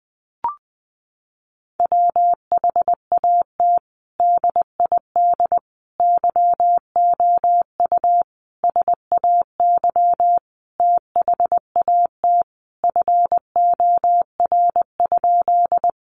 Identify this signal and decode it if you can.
Morse code, heard as a keyed tone, about 19 wpm, 700 Hz. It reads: WHAT DID YOU SAY THAT FOR?
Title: Sentences from Top 200 Words 20wpm